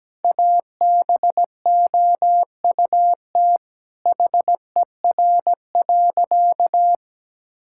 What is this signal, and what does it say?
Morse code, heard as a keyed tone, about 17 wpm, 700 Hz. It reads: ABOUT HER.
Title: Sentences from Top 1000 Words - Review Part 5 - 17wpm